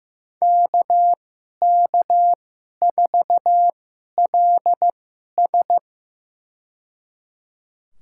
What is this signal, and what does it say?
Morse code, heard as a keyed tone, about 15 wpm, spaced as for 11 wpm, 700 Hz. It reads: KK4LS
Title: USA Call Signs - 5 Characters - 2x Character Spacing - 15wpm